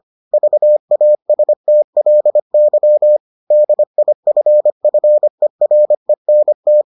Morse code, heard as a keyed tone, about 25 wpm, 600 Hz. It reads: VASTLY DIFFERENT